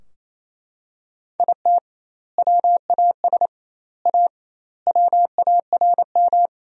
Morse code, heard as a keyed tone, about 28 wpm, 700 Hz. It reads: IT WAS A WARM
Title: 2x Word-Spacing - Sentences from Top 500 Words - 28wpm